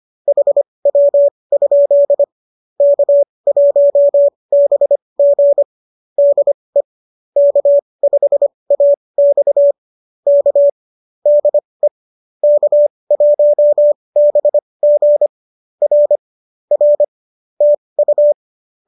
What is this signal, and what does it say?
Morse code, heard as a keyed tone, about 25 wpm, spaced as for 20 wpm, 570 Hz. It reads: HW? K1BG DE K5AX K DE K1BG R R TU
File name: qso201 20